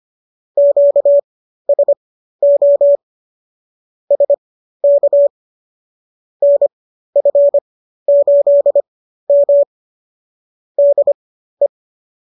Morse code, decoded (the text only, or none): QSO SK NF8M DE